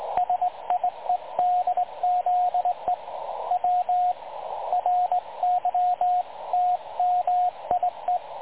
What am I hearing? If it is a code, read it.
Morse code, heard as a keyed tone, about 18 wpm, 720 Hz, with radio noise. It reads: SIEDZE W RYTMIE
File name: dlazaawansowanych / nagraniazpasma / websdr recording 3565.0kHz